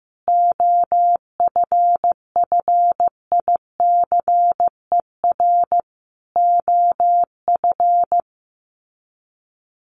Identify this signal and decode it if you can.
Morse code, heard as a keyed tone, about 15 wpm, 700 Hz. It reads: OFFICER OF
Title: Sets of 5 Words Encore2 15wpm